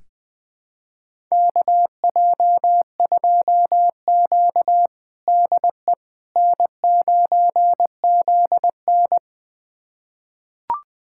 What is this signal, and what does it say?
Morse code, heard as a keyed tone, about 21 wpm, 700 Hz. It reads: KJ2Q DE N9ZN
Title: USA Callsign DE Callsign - 4 Characters - 20wpm